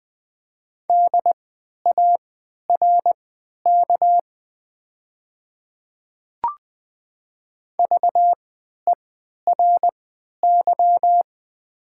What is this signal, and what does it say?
Morse code, heard as a keyed tone, about 20 wpm, spaced as for 11 wpm, 700 Hz. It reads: DARK VERY
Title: 3x Character Spacing - Top 500 Words - 20wpm